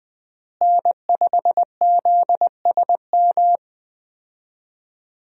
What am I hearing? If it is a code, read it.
Morse code, heard as a keyed tone, about 20 wpm, 700 Hz. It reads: N5ZSM
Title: USA Call Signs - 5 Characters - Repeated 3x - 20wpm